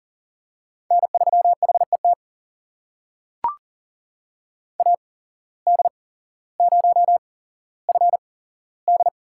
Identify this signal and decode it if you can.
Morse code, heard as a keyed tone, about 37 wpm, spaced as for 17 wpm, 700 Hz. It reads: N3HET AD0FD
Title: USA Call Signs - 8x Character Spacing - 40wpm